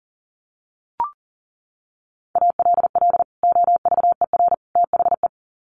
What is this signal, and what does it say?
Morse code, heard as a keyed tone, about 40 wpm, 700 Hz. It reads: ALL OVER THE
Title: Sets of 3 Words 40wpm